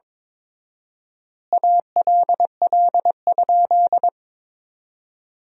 Morse code, heard as a keyed tone, about 22 wpm, 700 Hz. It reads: ALL?
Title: 4x Word-Spacing - Sentences from Top 100 Words - 22wpm